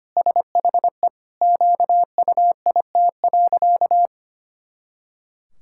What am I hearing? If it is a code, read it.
Morse code, heard as a keyed tone, about 25 wpm, 700 Hz. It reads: SHE QUIT.
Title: Pangram Challenge v2 - 25wpm